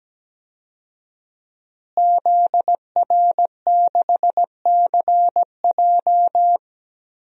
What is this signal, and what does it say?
Morse code, heard as a keyed tone, about 17 wpm, 700 Hz. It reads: ZR6CJ
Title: DXCC-100 Call Signs - 5 Characters - Repeated 2x - 17wpm